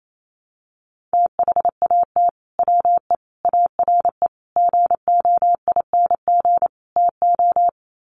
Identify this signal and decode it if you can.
Morse code, heard as a keyed tone, about 28 wpm, 700 Hz. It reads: THAT WE ARE GOING TO
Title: Sets of 5 Words Encore 28wpm